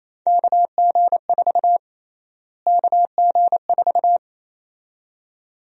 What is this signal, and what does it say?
Morse code, heard as a keyed tone, about 28 wpm, 700 Hz. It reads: KG4 KG4
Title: USA Call Sign Prefixes - Repeated 2x - 28wpm